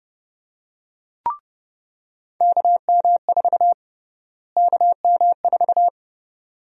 Morse code, heard as a keyed tone, about 32 wpm, 700 Hz. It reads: KM4 KM4
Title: USA Call Sign Prefixes - Repeated 2x - 30wpm